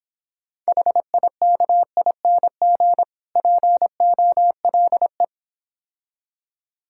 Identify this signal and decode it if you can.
Morse code, heard as a keyed tone, about 26 wpm, 700 Hz. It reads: HIKING POLE